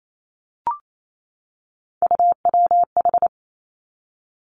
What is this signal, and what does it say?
Morse code, heard as a keyed tone, about 28 wpm, 700 Hz. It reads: UWH
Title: Three Challenging Letter-Number Combinations 28wpm